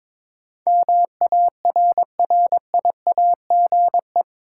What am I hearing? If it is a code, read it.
Morse code, heard as a keyed tone, about 21 wpm, 700 Hz. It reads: MARRIAGE
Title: Sentences from Top 800 Words - New Word Drill - 22wpm